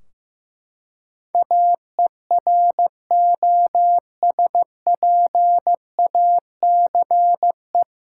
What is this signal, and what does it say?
Morse code, heard as a keyed tone, about 15 wpm, 700 Hz. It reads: AEROSPACE
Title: Practice QSO - Exchange 3 - More Info - 15wpm